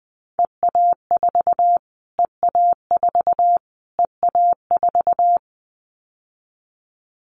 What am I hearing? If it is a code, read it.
Morse code, heard as a keyed tone, about 19 wpm, 700 Hz. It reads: EA4 EA4 EA4
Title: Callsign Prefixes for Top 25 DXCC - Repeated 3x - 20wpm